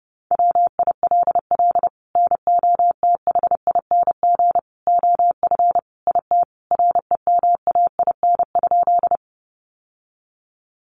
Morse code, heard as a keyed tone, about 30 wpm, 700 Hz. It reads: WILL NOTHING OF IT REMAIN?
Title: Sentences from Top 1200 Words 30wpm